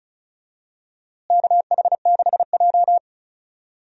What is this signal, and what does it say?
Morse code, heard as a keyed tone, about 35 wpm, 700 Hz. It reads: KH6J
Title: DXCC-100 Call Signs - 4 Characters - Repeated 2x - 35wpm